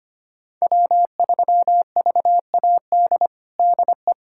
Morse code, heard as a keyed tone, about 25 wpm, 700 Hz. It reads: W3VAD DE
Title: USA Callsign DE Callsign - 5 Characters - 25wpm